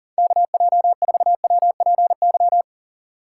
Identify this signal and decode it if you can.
Morse code, heard as a keyed tone, about 38 wpm, 700 Hz. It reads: KJ4WPY
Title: USA Call Signs - 6 Characters - 3x Character Spacing - 40wpm